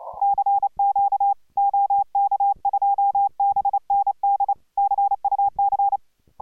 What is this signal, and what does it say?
Morse code, heard as a keyed tone, about 28 wpm, 810 Hz. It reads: CQ OK2BND CUC